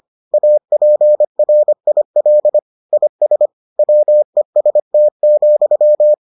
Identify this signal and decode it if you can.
Morse code, heard as a keyed tone, about 25 wpm, 600 Hz. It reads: APRIL IS WEST,